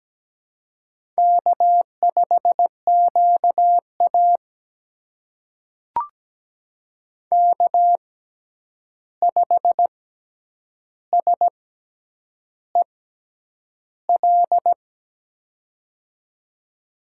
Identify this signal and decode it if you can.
Morse code, heard as a keyed tone, about 17 wpm, spaced as for 8 wpm, 700 Hz. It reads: K5QA K5SEL
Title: USA Call Signs - 6x Character Spacing - 17wpm